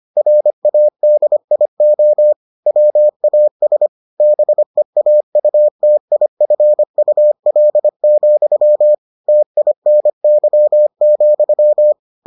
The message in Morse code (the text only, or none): RADIO WAS BEAUTIFUL, TINY,